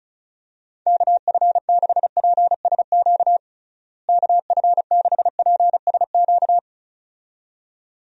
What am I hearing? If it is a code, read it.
Morse code, heard as a keyed tone, about 34 wpm, 700 Hz. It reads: KF6PSQ KF6PSQ
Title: USA Call Signs - 6 Characters - Repeated 2x - 35wpm